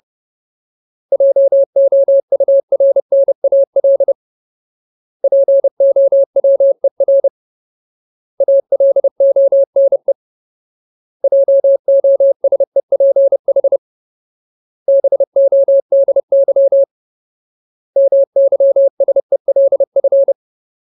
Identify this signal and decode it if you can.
Morse code, heard as a keyed tone, about 30 wpm, 550 Hz. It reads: JOURNAL POWER ALONE JOSEPH BODY MYSELF
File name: Ultimate Word List / 30WPM Ultimate